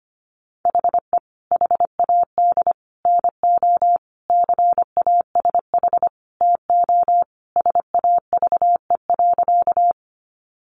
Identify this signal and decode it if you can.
Morse code, heard as a keyed tone, about 25 wpm, 690 Hz. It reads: HE HAD NO CASH TO SAVE.
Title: Sentences from Top 1100 Words 25wpm